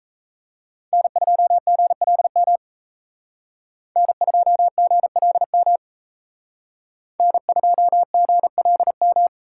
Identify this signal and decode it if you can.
Morse code, heard as a keyed tone, about 33 wpm, 700 Hz. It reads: N2GLM N2GLM N2GLM
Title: Speed Racing - USA 1x3 Call Signs 28wpm